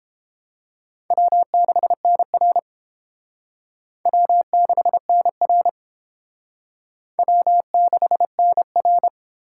Morse code, heard as a keyed tone, about 28 wpm, 700 Hz. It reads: W6NR W6NR W6NR